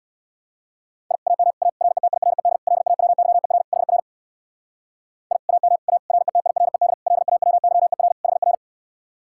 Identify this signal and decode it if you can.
Morse code, heard as a keyed tone, about 64 wpm, 700 Hz. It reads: I AM A RETIRED LAWYER HR I AM A RETIRED LAWYER HR